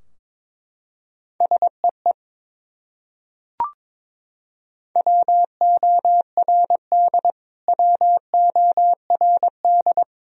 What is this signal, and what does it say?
Morse code, heard as a keyed tone, about 22 wpm, 700 Hz. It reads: SEE WORD WORD
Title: Top 100 Words Repeated 22wpm